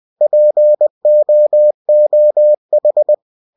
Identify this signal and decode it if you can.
Morse code, heard as a keyed tone, about 20 wpm, 600 Hz. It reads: POOH
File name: cw-books / winnie-the-pooh / winnie-20wpm-page-0091